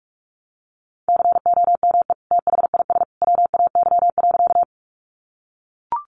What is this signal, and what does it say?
Morse code, heard as a keyed tone, about 44 wpm, 700 Hz. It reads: COME THIS WAY.
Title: Sentences from Top 100 Words 45wpm